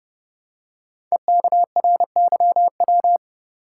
Morse code, heard as a keyed tone, about 30 wpm, 700 Hz. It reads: EKRYW
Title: Code Groups - 5 Letters - 30wpm